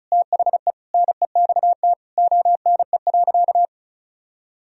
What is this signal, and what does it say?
Morse code, heard as a keyed tone, about 36 wpm, 700 Hz. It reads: THE NEXT ONE.
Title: Sentences from Top 400 Words and Abbreviations 35wpm